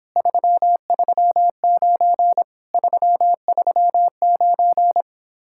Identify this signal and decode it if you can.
Morse code, heard as a keyed tone, about 29 wpm, 700 Hz. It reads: 339 339